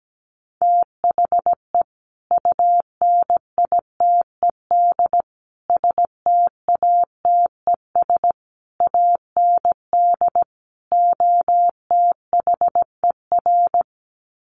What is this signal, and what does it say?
Morse code, heard as a keyed tone, about 17 wpm, 700 Hz. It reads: THE UNITED STATES AND OTHER
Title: Sets of 5 Words Encore 17wpm